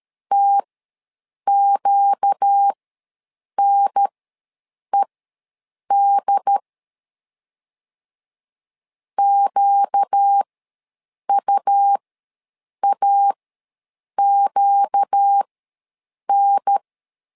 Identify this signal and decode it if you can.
Morse code, heard as a keyed tone, about 13 wpm, spaced as for 7 wpm, 800 Hz. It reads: TQNED QUAQN
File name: LEZIONI / 4 Q ws 12wpm 1912